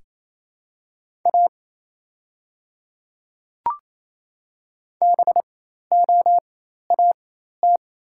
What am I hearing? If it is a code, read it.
Morse code, heard as a keyed tone, about 28 wpm, spaced as for 13 wpm, 700 Hz. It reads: A BOAT
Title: 4x Character Spacing - Top 500 Words - 28wpm